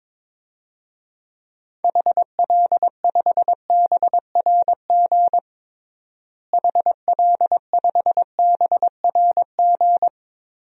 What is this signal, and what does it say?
Morse code, heard as a keyed tone, about 22 wpm, 700 Hz. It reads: HL5BRG HL5BRG
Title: DXCC-100 Call Signs - 6 Characters - Repeated 3x - 22wpm